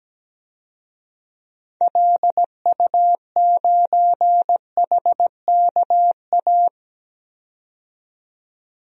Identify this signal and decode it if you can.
Morse code, heard as a keyed tone, about 17 wpm, 700 Hz. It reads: LU9HKA
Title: DXCC-100 Call Signs - 6 Characters - 17wpm